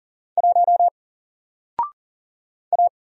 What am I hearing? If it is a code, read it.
Morse code, heard as a keyed tone, about 40 wpm, 700 Hz. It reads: 1 A
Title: Single Letter-Number Alt 40wpm